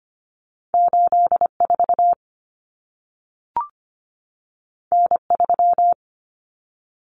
Morse code, heard as a keyed tone, about 25 wpm, 700 Hz. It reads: 84 N3
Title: Common Cut Numbers - 2-Digit 25wpm